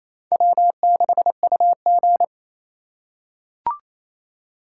Morse code, heard as a keyed tone, about 28 wpm, 700 Hz. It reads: W6UG
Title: USA Call Signs - 4 Characters - 4x Character Spacing - 28wpm